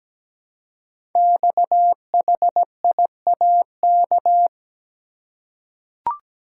Morse code, heard as a keyed tone, about 17 wpm, 700 Hz. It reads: XHIAK
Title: Code Groups - 5 Letter:Numbers - 17wpm